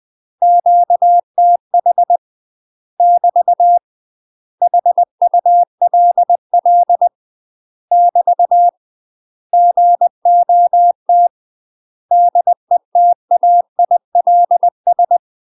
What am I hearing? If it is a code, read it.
Morse code, heard as a keyed tone, about 20 wpm, 700 Hz. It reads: QTH <BT> HULL <BT> GOT DETAILS
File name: UK MASTER QSOs TRAINING / 20WPM / UK MASTER QSOs 0010